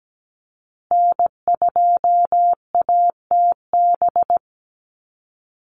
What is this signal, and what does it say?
Morse code, heard as a keyed tone, about 17 wpm, 700 Hz. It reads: N2ATB
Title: Call Signs Repeated 3x - 17wpm